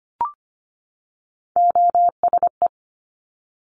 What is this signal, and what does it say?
Morse code, heard as a keyed tone, about 25 wpm, 700 Hz. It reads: OSE
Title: Top Three Letter Combinations in Words 25wpm